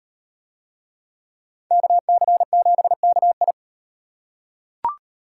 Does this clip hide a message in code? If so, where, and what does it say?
Yes: Morse code from 1.7 to 3.5 s, heard as a keyed tone, about 35 wpm, 700 Hz. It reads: KC7KI